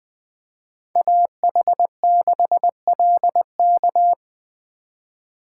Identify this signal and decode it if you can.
Morse code, heard as a keyed tone, about 19 wpm, 700 Hz. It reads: AH6LK